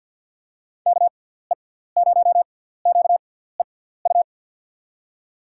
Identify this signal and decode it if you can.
Morse code, heard as a keyed tone, about 48 wpm, spaced as for 17 wpm, 700 Hz. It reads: KE0QEU
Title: USA Call Signs - 6 Characters - 6x Character Spacing - 50wpm